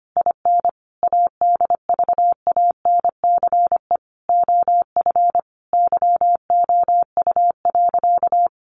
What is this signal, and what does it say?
Morse code, heard as a keyed tone, about 25 wpm, 700 Hz. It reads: IN ADVANCE OF YOU.
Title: Sentences from Top 1400 Words 25wpm